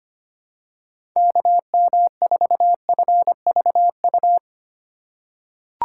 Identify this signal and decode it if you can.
Morse code, heard as a keyed tone, about 25 wpm, 700 Hz. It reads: KM4FVU
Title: USA Call Signs - 4x Character Spacing - 25wpm